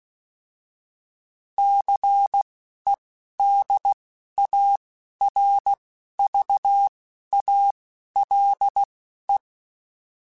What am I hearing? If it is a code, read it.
Morse code, heard as a keyed tone, about 16 wpm, spaced as for 11 wpm, 800 Hz. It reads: CEDARVALE
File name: fromSMs / FromSMs mp3s / Cities@10